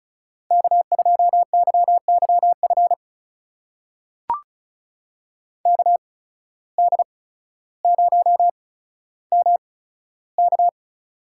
Morse code, heard as a keyed tone, about 36 wpm, spaced as for 15 wpm, 700 Hz. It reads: K2YYF KD0MK
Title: USA Call Signs - 8x Character Spacing - 35wpm